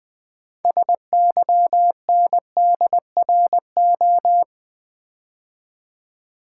Morse code, heard as a keyed tone, about 20 wpm, 700 Hz. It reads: SYNDRO
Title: Build A Word - 2 Letters At A Time - 8-Letter Words - 20wpm